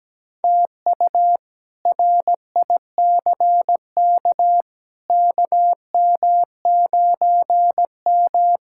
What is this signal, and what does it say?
Morse code, heard as a keyed tone, about 17 wpm, 700 Hz. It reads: TU RICK KM9M
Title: K1USN SST QSO Practice - 17wpm